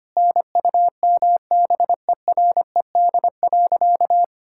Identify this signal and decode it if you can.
Morse code, heard as a keyed tone, about 25 wpm, 700 Hz. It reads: NUMBERED.
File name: practice / 25wpm / All-Age Appropriate Jokes - 25wpm